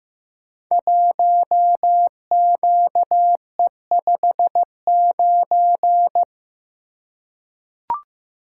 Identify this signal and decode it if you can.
Morse code, heard as a keyed tone, about 15 wpm, 700 Hz. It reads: 1QE59
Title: Code Groups - 5 Letter:Numbers - 15wpm